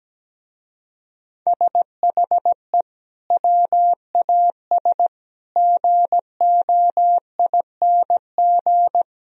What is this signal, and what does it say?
Morse code, heard as a keyed tone, about 17 wpm, 700 Hz. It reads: SHE WAS GOING
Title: Speed-Racing - Sets of 4 Words 17wpm